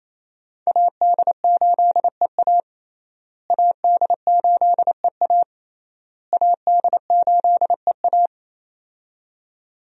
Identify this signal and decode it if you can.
Morse code, heard as a keyed tone, about 28 wpm, 700 Hz. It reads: AD8EA AD8EA AD8EA
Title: USA Call Signs - 5 Characters - Repeated 3x - 28wpm